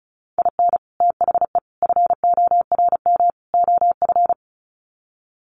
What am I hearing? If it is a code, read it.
Morse code, heard as a keyed tone, about 34 wpm, 700 Hz. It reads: IN THE FORM OF
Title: Sets of 4 Words 35wpm